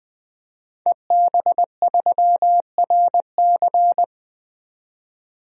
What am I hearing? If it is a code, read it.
Morse code, heard as a keyed tone, about 20 wpm, 700 Hz. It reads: EB3RC
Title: DXCC-100 Call Signs - 5 Characters - 20wpm